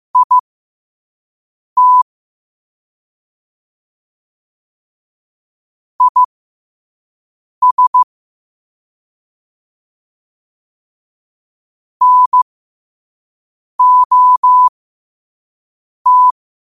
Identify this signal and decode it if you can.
Morse code, heard as a keyed tone, about 15 wpm, spaced as for 5 wpm, 1000 Hz. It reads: IT IS NOT
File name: 5wpm